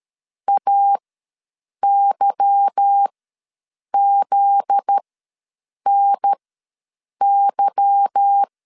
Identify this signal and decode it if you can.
Morse code, heard as a keyed tone, about 13 wpm, spaced as for 7 wpm, 800 Hz. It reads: AYZNY
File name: LEZIONI / 8 YJ45 / 8 Y ws 12wpm 3409